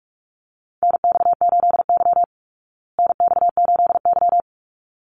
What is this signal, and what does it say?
Morse code, heard as a keyed tone, about 44 wpm, 700 Hz. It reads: NX8Y NX8Y
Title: Call Signs Repeated 3x - 45wpm